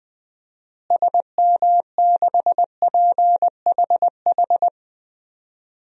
Morse code, heard as a keyed tone, about 20 wpm, 700 Hz. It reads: SM6PHH
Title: DXCC-100 Call Signs - 6 Characters - Repeated 2x - 20wpm